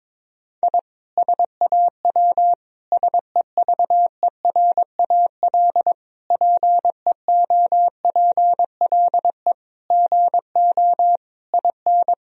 Morse code, heard as a keyed tone, about 22 wpm, 700 Hz. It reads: I SAW SEVERAL PEOPLE GO IN
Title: Sentences from Top 1000 Words - Review Part 2 - 22wpm